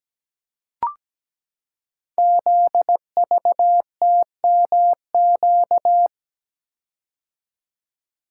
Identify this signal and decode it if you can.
Morse code, heard as a keyed tone, about 17 wpm, 700 Hz. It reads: ZVTMQ
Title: Code Groups - 5 Letters - 17wpm